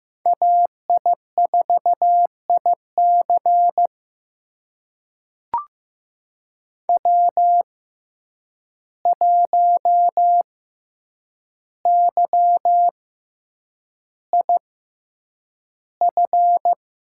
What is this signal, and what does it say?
Morse code, heard as a keyed tone, about 15 wpm, spaced as for 8 wpm, 700 Hz. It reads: AI4IC W1YIF
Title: USA Call Signs - 5 Characters - 6x Character Spacing - 15wpm